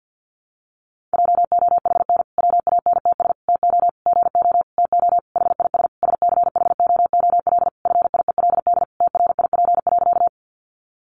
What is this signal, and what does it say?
Morse code, heard as a keyed tone, about 50 wpm, 700 Hz. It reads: JOHN WANTS TO GO TO HIS SCHOOL FIELD TRIP.
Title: Additional Easy Phrases 50wpm